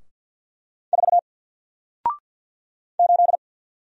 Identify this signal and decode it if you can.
Morse code, heard as a keyed tone, about 48 wpm, 700 Hz. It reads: 4 8
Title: Single Character (Letter:Number:Punctuation:Prosign) Alt 50wpm